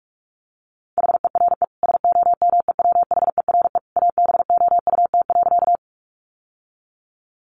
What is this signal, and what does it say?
Morse code, heard as a keyed tone, about 44 wpm, 700 Hz. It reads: HERE SOMEWHERE ABOUT.
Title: Sentences from Top 1400 Words 45wpm